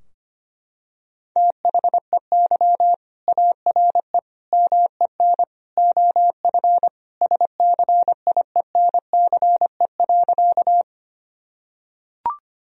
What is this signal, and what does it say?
Morse code, heard as a keyed tone, about 25 wpm, 700 Hz. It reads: THEY ARE MEN OF SCIENCE.
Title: Sentences from Top 1000 Words - Review Part 2 - 25wpm